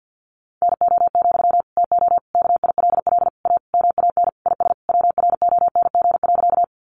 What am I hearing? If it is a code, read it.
Morse code, heard as a keyed tone, about 50 wpm, 700 Hz. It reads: NO, TO KILL A MAN IS WRONG.
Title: Sentences from Top 1400 Words 50wpm